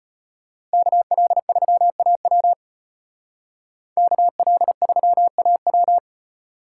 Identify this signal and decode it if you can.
Morse code, heard as a keyed tone, about 33 wpm, 700 Hz. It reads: KL3AW KL3AW